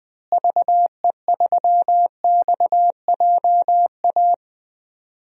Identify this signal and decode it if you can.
Morse code, heard as a keyed tone, about 20 wpm, 700 Hz. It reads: VE3XJA
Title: DXCC-100 Call Signs - 6 Characters - Repeated 2x - 20wpm